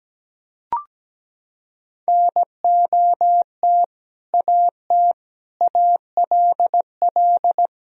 Morse code, heard as a keyed tone, about 17 wpm, 700 Hz. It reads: NOT AT ALL
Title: Sentences from Top 1000 Words - Review Part 4 - 17wpm